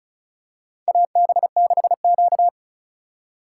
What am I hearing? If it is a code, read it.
Morse code, heard as a keyed tone, about 36 wpm, 700 Hz. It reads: AB6Q
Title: USA Call Signs - 4x Character Spacing - 35wpm